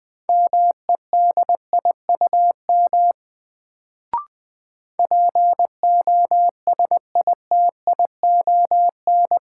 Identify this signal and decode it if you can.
Morse code, heard as a keyed tone, about 21 wpm, 700 Hz. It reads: MEDIUM POSITION
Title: Sentences from Top 900 Words - New Word Drill - 20wpm